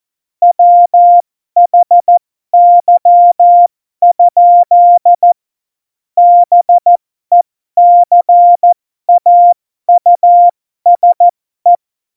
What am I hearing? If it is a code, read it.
Morse code, heard as a keyed tone, about 14 wpm, spaced as for 12 wpm, 700 Hz. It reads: WHY? BECAUSE
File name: Morsekurs / Witze